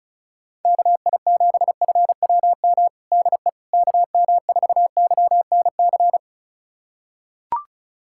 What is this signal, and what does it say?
Morse code, heard as a keyed tone, about 34 wpm, 700 Hz. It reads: KI7FWM DE KM4YNC
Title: USA Callsign DE Callsign - 6 Characters - 35wpm